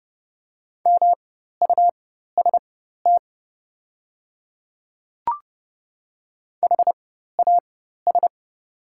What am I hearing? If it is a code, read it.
Morse code, heard as a keyed tone, about 30 wpm, spaced as for 14 wpm, 700 Hz. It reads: MUST HAS
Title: 4x Character Spacing - Top 500 Words - 30wpm